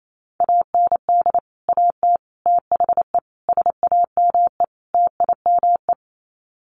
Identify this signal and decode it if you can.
Morse code, heard as a keyed tone, about 28 wpm, 700 Hz. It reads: AND AT THE SAME TIME
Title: Sets of 5 Words 28wpm